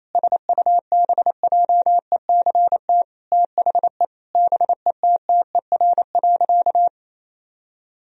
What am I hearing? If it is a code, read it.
Morse code, heard as a keyed tone, about 28 wpm, 700 Hz. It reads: SUBJECT THE BETTER.
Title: Sentences from Top 1000 Words - Review Part 2 - 28wpm